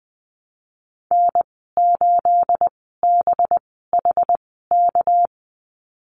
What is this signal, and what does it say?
Morse code, heard as a keyed tone, about 20 wpm, spaced as for 15 wpm, 700 Hz. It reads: N8BHK
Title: 213 Lesson 34g - Number 8 - Call Signs 20wpm